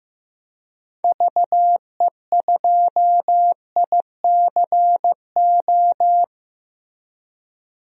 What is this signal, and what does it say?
Morse code, heard as a keyed tone, about 15 wpm, 700 Hz. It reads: VE2ICO
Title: DXCC-100 Call Signs - 6 Characters - 15wpm